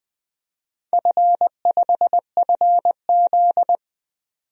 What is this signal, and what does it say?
Morse code, heard as a keyed tone, about 20 wpm, 700 Hz. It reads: F5FZ
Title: DXCC-100 Call Signs - 4 Characters - 20wpm